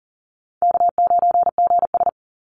Morse code, heard as a keyed tone, about 40 wpm, 700 Hz. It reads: K9GS
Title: Call Signs Repeated 3x - 40wpm